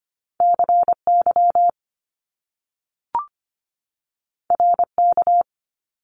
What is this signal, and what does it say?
Morse code, heard as a keyed tone, about 25 wpm, 700 Hz. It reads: CY RK
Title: Top Two Letter Combinations in Words 25wpm